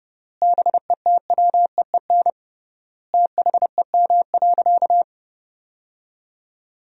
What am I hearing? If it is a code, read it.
Morse code, heard as a keyed tone, about 30 wpm, 700 Hz. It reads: BETWEEN THEM.
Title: 3x Word-Spacing - Sentences from Top 400 Words - 30wpm